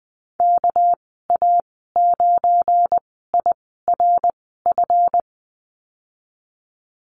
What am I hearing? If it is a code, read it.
Morse code, heard as a keyed tone, about 20 wpm, 700 Hz. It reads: K A 9 I R F
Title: 238 Lesson 38g - Letter q - Call Signs 20wpm